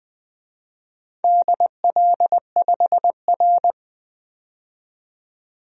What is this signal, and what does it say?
Morse code, heard as a keyed tone, about 21 wpm, 700 Hz. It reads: DL5R
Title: DXCC-100 Call Signs - 4 Characters - 20wpm